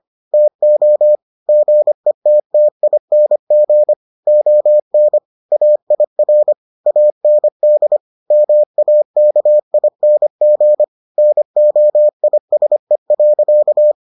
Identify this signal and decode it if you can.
Morse code, heard as a keyed tone, about 25 wpm, 600 Hz. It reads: TO GETTING ON AIR AND MAKING NOISE.